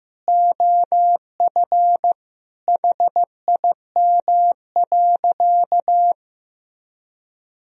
Morse code, heard as a keyed tone, about 15 wpm, 700 Hz. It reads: OF HIM.
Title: Sentences from Top 1000 Words - Review Part 4 - 15wpm